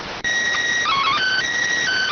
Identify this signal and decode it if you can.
SSTV (slow-scan television), PD 290.